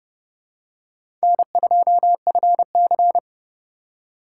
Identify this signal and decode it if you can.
Morse code, heard as a keyed tone, about 30 wpm, 700 Hz. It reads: N2FC